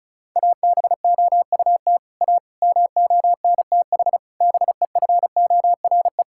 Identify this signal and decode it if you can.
Morse code, heard as a keyed tone, about 34 wpm, 700 Hz. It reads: ABOUT A MONTH BEFORE
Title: Sentences from Top 1000 Words - Review Part 5 - 35wpm